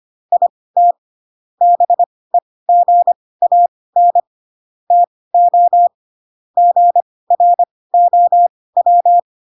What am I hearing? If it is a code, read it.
Morse code, heard as a keyed tone, about 25 wpm, spaced as for 18 wpm, 700 Hz. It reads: IT BEGAN TO GROW
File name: Pinocchio ch3 12 25 ik1hge